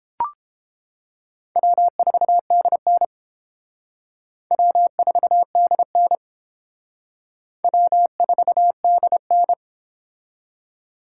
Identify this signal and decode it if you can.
Morse code, heard as a keyed tone, about 31 wpm, 700 Hz. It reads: W4DN W4DN W4DN